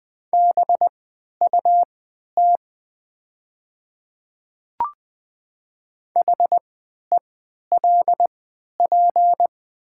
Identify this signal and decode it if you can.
Morse code, heard as a keyed tone, about 20 wpm, spaced as for 11 wpm, 700 Hz. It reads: BUT HELP
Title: 3x Character Spacing - Top 200 Words - 20wpm